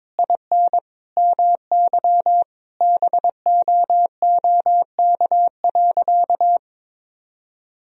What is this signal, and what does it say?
Morse code, heard as a keyed tone, about 22 wpm, 700 Hz. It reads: IN MY BOOK.
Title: Sentences from Top 1000 Words - Review Part 1 - 22wpm